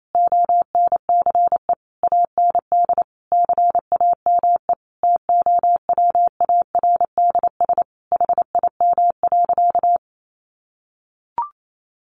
Morse code, heard as a keyed tone, about 28 wpm, 700 Hz. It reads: ONCE AND CAME TOWARDS HIM.
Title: Sentences from Top 1100 Words 28wpm